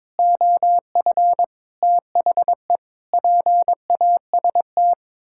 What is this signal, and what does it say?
Morse code, heard as a keyed tone, about 22 wpm, 700 Hz. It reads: OF THE PAST
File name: practice / 22wpm / Sets of 6 Words - 22wpm